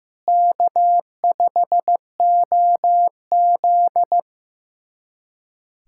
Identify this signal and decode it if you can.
Morse code, heard as a keyed tone, about 16 wpm, 700 Hz. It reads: K5OZ
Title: USA Callsign DE Callsign - 4 Characters - 15wpm